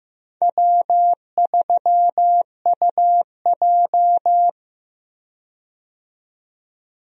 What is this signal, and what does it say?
Morse code, heard as a keyed tone, about 15 wpm, 700 Hz. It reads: W3UJ
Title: USA Call Signs - 4 Characters - 15wpm